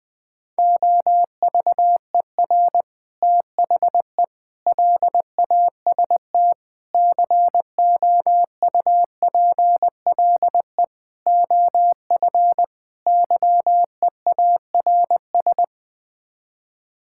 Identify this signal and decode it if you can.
Morse code, heard as a keyed tone, about 20 wpm, 700 Hz. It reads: OVER THE LAST COUPLE OF YEARS
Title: Sets of 6 Words - 20wpm